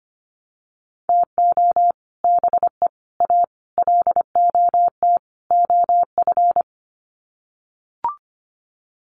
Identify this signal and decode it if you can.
Morse code, heard as a keyed tone, about 25 wpm, 700 Hz. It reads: TO BE A LOT OF
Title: Sets of 5 Words Encore 25wpm